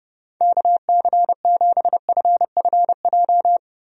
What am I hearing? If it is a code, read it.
Morse code, heard as a keyed tone, about 32 wpm, 700 Hz. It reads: KC7FFJ
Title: USA Call Signs - 6 Characters - Repeated 2x - 30wpm